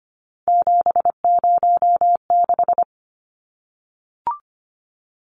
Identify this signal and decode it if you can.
Morse code, heard as a keyed tone, about 25 wpm, 700 Hz. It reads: 706
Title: Numbers - 3-Digit 25wpm